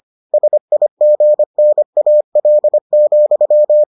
Morse code, heard as a keyed tone, about 25 wpm, 600 Hz. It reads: SIGNAL,